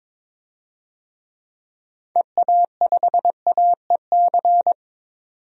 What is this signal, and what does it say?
Morse code, heard as a keyed tone, about 22 wpm, 700 Hz. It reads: EA5AEC
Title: DXCC-100 Call Signs - 6 Characters - 22wpm